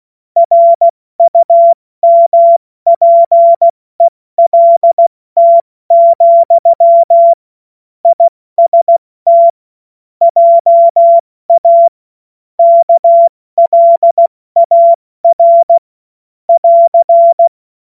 Morse code, heard as a keyed tone, about 16 wpm, 680 Hz. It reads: RUMPELT, IST JA KLAR <AR>
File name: Morsekurs / Witze / Witz 21. Juni 2022 70 F80